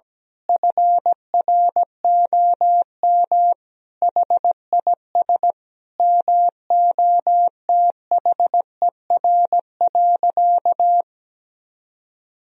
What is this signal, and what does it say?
Morse code, heard as a keyed tone, about 17 wpm, 700 Hz. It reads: FROM HIS MOTHER.
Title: Sentences from Top 1000 Words - Review Part 4 - 17wpm